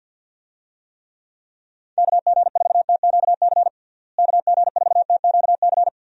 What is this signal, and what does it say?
Morse code, heard as a keyed tone, about 50 wpm, 700 Hz. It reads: KG4TQC KG4TQC
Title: USA Call Signs - 6 Characters - Repeated 2x - 50wpm